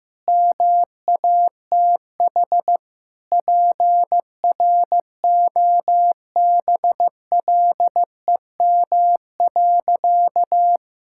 Morse code, heard as a keyed tone, about 15 wpm, 700 Hz. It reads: MATH PROBLEM.
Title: Sentences from Top 1000 Words - Review Part 6 - 15wpm